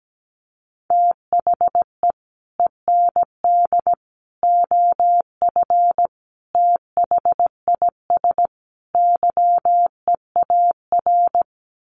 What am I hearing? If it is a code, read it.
Morse code, heard as a keyed tone, about 17 wpm, 700 Hz. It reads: THE END OF THIS YEAR
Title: Sets of 5 Words Encore 17wpm